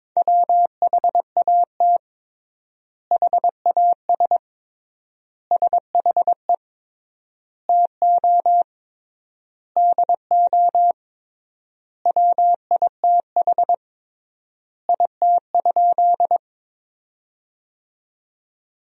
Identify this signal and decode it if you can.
Morse code, heard as a keyed tone, about 21 wpm, 700 Hz. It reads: WHAT HAS SHE TO DO WITH IT?
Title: 3x Word-Spacing - Sentences from Top 100 Words - 22wpm